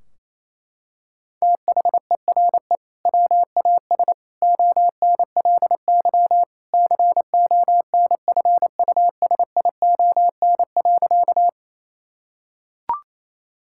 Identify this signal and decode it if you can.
Morse code, heard as a keyed tone, about 27 wpm, 700 Hz. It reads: THERE WAS ONLY CONFUSION.
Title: Sentences from Top 1600 Words 28wpm